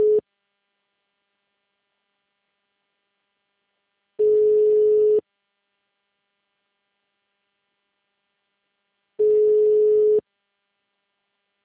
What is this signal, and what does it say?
Ringback tone (Europe, 425 Hz).
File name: out- -2009- - - .11396